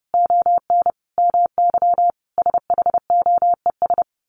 Morse code, heard as a keyed tone, about 30 wpm, 700 Hz. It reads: ON MY SHOES
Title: A Sound Of Thunder 30wpm